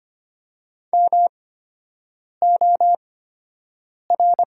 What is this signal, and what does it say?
Morse code, heard as a keyed tone, about 25 wpm, spaced as for 7 wpm, 700 Hz. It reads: MOR
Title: 8x Character Spacing - Top 100 Words - 25wpm